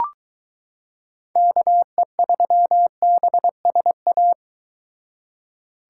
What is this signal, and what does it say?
Morse code, heard as a keyed tone, about 23 wpm, 700 Hz. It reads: KE3BSA